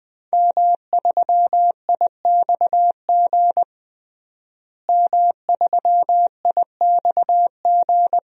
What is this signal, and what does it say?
Morse code, heard as a keyed tone, about 21 wpm, 700 Hz. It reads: M3IXG M3IXG
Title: DXCC-100 Call Signs - 5 Characters - Repeated 3x - 20wpm